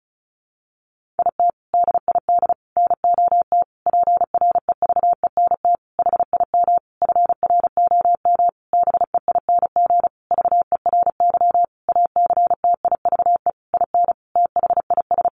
Morse code, heard as a keyed tone, about 34 wpm, 700 Hz. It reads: IT DID NOT PREVENT HIM FROM BEING VERY ACTIVE IN THIS
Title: Sentences from Top 1500 Words 35wpm